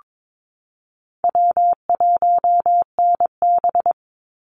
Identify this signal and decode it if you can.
Morse code, heard as a keyed tone, about 22 wpm, 700 Hz. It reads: W1NB
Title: USA Call Signs 22wpm